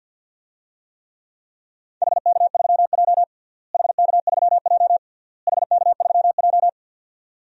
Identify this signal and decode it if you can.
Morse code, heard as a keyed tone, about 53 wpm, 700 Hz. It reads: HK3J HK3J HK3J
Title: DXCC-100 Call Signs - 4 Characters - Repeated 3x - 50wpm